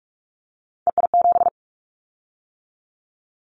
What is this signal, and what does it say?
Morse code, heard as a keyed tone, about 46 wpm, 700 Hz. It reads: EI7
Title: Callsign Prefixes for Top 50 DXCC - 45wpm